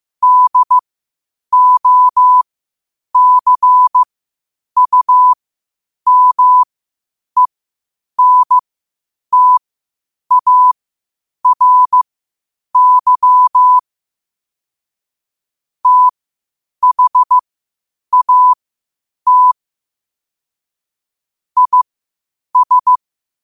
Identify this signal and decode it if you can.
Morse code, heard as a keyed tone, about 15 wpm, spaced as for 8 wpm, 1000 Hz. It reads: DOCUMENTARY THAT IS
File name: 8wpm